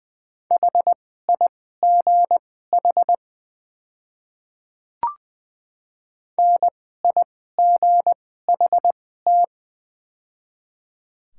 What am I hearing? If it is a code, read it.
Morse code, heard as a keyed tone, about 20 wpm, spaced as for 15 wpm, 700 Hz. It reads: HIGH NIGHT
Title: 2x Character Spacing - Top 300 Words - 20wpm